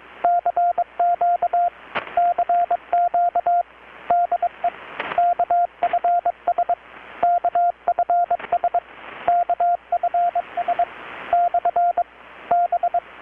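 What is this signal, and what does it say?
Morse code, heard as a keyed tone, about 21 wpm, 720 Hz, with radio noise. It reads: CQ CQ DE KFS KFS KFS / B
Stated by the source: Morse from KPH's Night of Nights XX maritime radiotelegraphy event, received in Chiapas, Mexico about 3,800 km away on a Kenwood R-600 with a grounded 10 m long-wire antenna.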